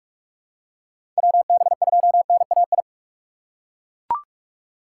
Morse code, heard as a keyed tone, about 42 wpm, 700 Hz. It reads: WB2NAI
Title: USA Call Signs - 6 Characters - 4x Character Spacing - 45wpm